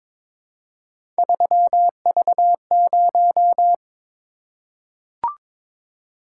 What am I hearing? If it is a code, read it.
Morse code, heard as a keyed tone, about 22 wpm, 700 Hz. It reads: 3V0
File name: practice / 22wpm / Three Letter-Number Combinations - 22wpm